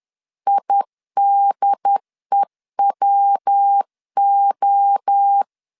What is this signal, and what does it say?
Morse code, heard as a keyed tone, about 10 wpm, 800 Hz. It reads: IDEWO
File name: CW / DET / 1 3 1500C S B 10wpm 3335